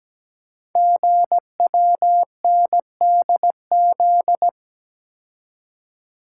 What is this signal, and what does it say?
Morse code, heard as a keyed tone, about 17 wpm, 700 Hz. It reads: GWNDZ
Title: Code Groups - 5 Letters - 17wpm